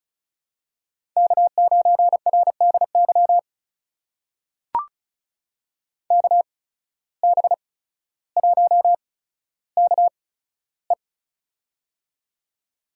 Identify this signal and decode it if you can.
Morse code, heard as a keyed tone, about 36 wpm, spaced as for 15 wpm, 700 Hz. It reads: K9RDY KB1KE
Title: USA Call Signs - 5 Characters - 8x Character Spacing - 35wpm